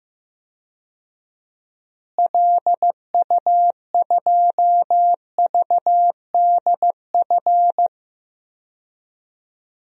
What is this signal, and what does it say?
Morse code, heard as a keyed tone, about 15 wpm, 700 Hz. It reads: LU2VDF
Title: DXCC-100 Call Signs - 6 Characters - Repeated 3x - 15wpm